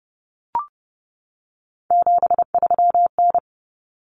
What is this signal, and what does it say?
Morse code, heard as a keyed tone, about 30 wpm, 700 Hz. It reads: 73N
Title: Common Cut Numbers - 3-Digit 30wpm